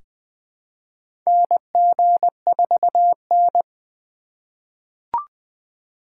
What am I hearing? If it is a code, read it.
Morse code, heard as a keyed tone, about 21 wpm, 700 Hz. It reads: NG4N
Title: USA Call Signs - 4 Characters - 3x Character Spacing - 20wpm